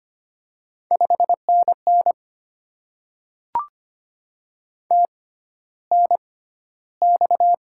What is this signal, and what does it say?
Morse code, heard as a keyed tone, about 25 wpm, spaced as for 13 wpm, 690 Hz. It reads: 5NN TNX
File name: Top 20 QSO Elements - 6x Character Spacing - 25wpm